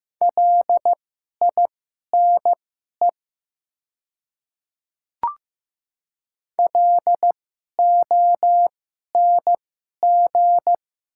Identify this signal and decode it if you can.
Morse code, heard as a keyed tone, about 15 wpm, spaced as for 11 wpm, 700 Hz. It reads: LINE LONG
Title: 2x Character Spacing - Top 300 Words - 15wpm